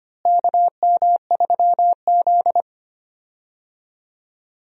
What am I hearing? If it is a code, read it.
Morse code, heard as a keyed tone, about 25 wpm, 700 Hz. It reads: KM3Z
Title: USA Call Signs - 4 Characters - Repeated 2x - 25wpm